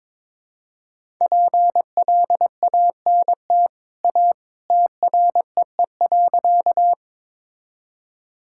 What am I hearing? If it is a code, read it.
Morse code, heard as a keyed tone, about 21 wpm, 700 Hz. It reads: PLANT A TREE.
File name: practice / 22wpm / Easy Sentences with Hints - 22wpm